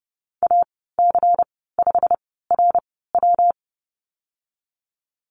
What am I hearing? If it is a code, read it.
Morse code, heard as a keyed tone, about 31 wpm, spaced as for 17 wpm, 700 Hz. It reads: AC5RW
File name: Lesson 19g - Letter w - Call Signs 30wpm